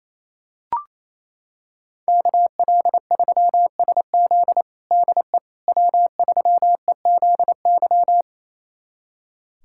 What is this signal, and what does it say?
Morse code, heard as a keyed tone, about 28 wpm, 700 Hz. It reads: KL3SZ DE W3EZY
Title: USA Callsign DE Callsign - 5 Characters - 28wpm